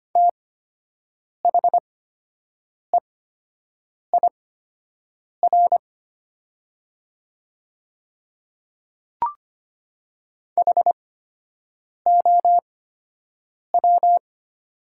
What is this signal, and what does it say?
Morse code, heard as a keyed tone, about 25 wpm, spaced as for 7 wpm, 700 Hz. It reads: THEIR HOW
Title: 8x Character Spacing - Top 100 Words - 25wpm